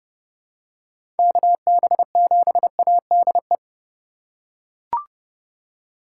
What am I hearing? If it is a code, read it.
Morse code, heard as a keyed tone, about 31 wpm, 700 Hz. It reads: KB7ADE
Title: USA Call Signs - 6 Characters - 3x Character Spacing - 30wpm